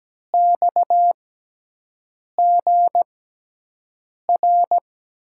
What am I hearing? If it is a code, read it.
Morse code, heard as a keyed tone, about 17 wpm, spaced as for 6 wpm, 700 Hz. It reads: XGR
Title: USA Call Signs - 5 Characters - 6x Character Spacing - 17wpm